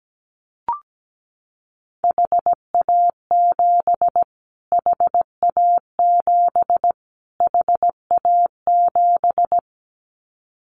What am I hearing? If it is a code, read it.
Morse code, heard as a keyed tone, about 17 wpm, 700 Hz. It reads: HA7 HA7 HA7
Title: Callsign Prefixes for Top 25 DXCC - Repeated 3x - 17wpm